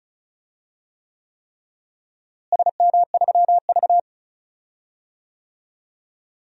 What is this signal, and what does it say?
Morse code, heard as a keyed tone, about 34 wpm, 700 Hz. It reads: SM3V
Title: DXCC-100 Call Signs - 4 Characters - Repeated 3x - 35wpm